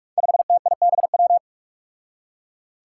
Morse code, heard as a keyed tone, about 46 wpm, 700 Hz. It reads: 5TIBW
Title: Five Challenging Letter-Number Combinations 45wpm